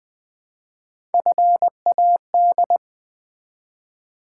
Morse code, heard as a keyed tone, about 20 wpm, 700 Hz. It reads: FAD
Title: Build A Word - 1 Letter At A Time - 4-Letter Words - 20wpm